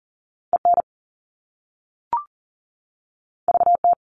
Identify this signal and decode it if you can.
Morse code, heard as a keyed tone, about 40 wpm, 700 Hz. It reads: EN VT
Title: All Cut Numbers - 2-Digit 40wpm